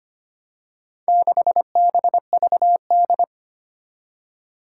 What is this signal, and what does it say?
Morse code, heard as a keyed tone, about 25 wpm, 700 Hz. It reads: 6BVD
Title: Four Challenging Letter-Number Combinations 25wpm